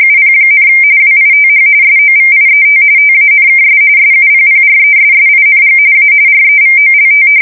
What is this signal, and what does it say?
RTTY, about 45.45 baud. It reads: UP, WITH SOVIET TANKS SUPPRESSING A REVOLT
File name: fsk170 / battleshipnj